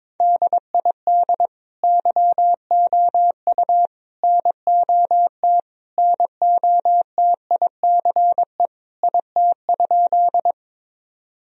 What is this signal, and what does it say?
Morse code, heard as a keyed tone, about 22 wpm, 700 Hz. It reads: DID YOU NOT NOTICE IT?
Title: Sentences from Top 1000 Words - Review Part 2 - 22wpm